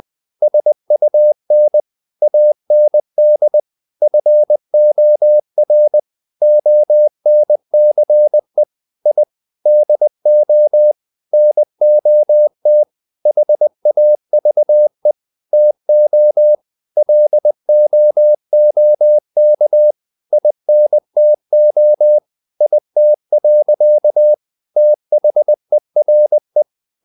Morse code, heard as a keyed tone, about 20 wpm, 600 Hz. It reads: SUN AND FOR ONCE I DO NOT HAVE TO LOOK INTO IT. THERE
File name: the old man and the sea / the old man and the sea part 1 20wpm